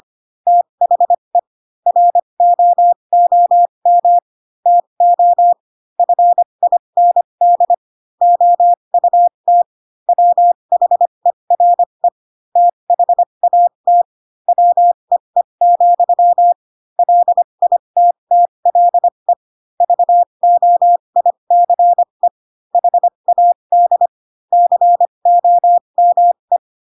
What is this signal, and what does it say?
Morse code, heard as a keyed tone, about 25 wpm, spaced as for 22 wpm, 700 Hz. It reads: THE ROOM TO FIND OUT WHERE THAT WEE, LITTLE VOICE HAD COME